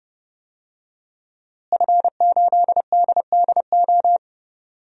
Morse code, heard as a keyed tone, about 29 wpm, 700 Hz. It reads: F8DDO
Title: DXCC-100 Call Signs - 5 Characters - Repeated 3x - 30wpm